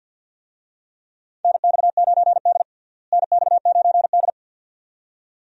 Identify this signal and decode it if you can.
Morse code, heard as a keyed tone, about 52 wpm, 700 Hz. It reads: NX9D NX9D
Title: USA Call Signs - 4 Characters - Repeated 2x - 50wpm